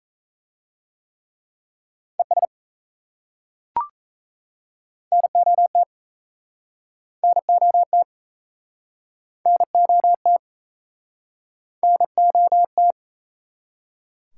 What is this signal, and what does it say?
Morse code, heard as a keyed tone, about 35 wpm, 700 Hz. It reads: ES NOT NOT NOT NOT